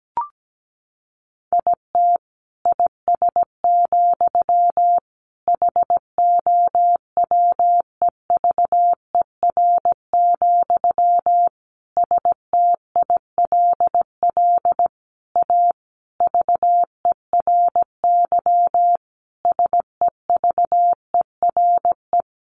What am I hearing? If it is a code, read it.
Morse code, heard as a keyed tone, about 17 wpm, 700 Hz. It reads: IT IS, HOWEVER, STILL A VERY SEVERE
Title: Sentences from Top 1400 Words 17wpm